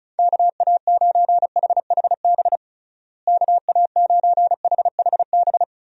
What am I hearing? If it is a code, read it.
Morse code, heard as a keyed tone, about 36 wpm, 700 Hz. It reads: KA9HHB KA9HHB
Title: USA Call Signs - 6 Characters - Repeated 3x - 35wpm